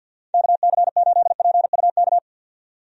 Morse code, heard as a keyed tone, about 52 wpm, 700 Hz. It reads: KC8PUK
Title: USA Call Signs - 6 Characters - 50wpm